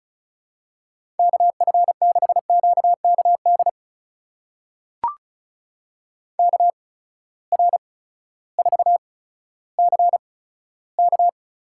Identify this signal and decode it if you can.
Morse code, heard as a keyed tone, about 36 wpm, spaced as for 16 wpm, 700 Hz. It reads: KF6QKD KR4CK
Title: USA Call Signs - 6 Characters - 8x Character Spacing - 35wpm